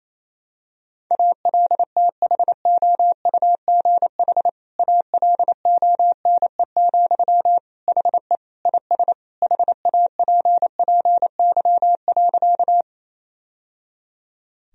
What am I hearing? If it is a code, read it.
Morse code, heard as a keyed tone, about 27 wpm, 700 Hz. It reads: ALTHOUGH ALONE, HE IS HAPPY.
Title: Sentences from Top 1000 Words - Review Part 3 - 28wpm